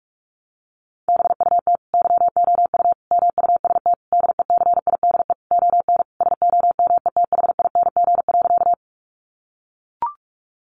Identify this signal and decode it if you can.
Morse code, heard as a keyed tone, about 44 wpm, 700 Hz. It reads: BUT YOU MUST DECIDE ON SOMETHING.
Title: Sentences from Top 600 Words 45wpm